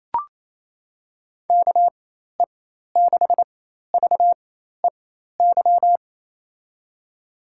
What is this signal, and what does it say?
Morse code, heard as a keyed tone, about 28 wpm, spaced as for 13 wpm, 700 Hz. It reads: KE6VEY
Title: USA Call Signs - 6 Characters - 4x Character Spacing - 28wpm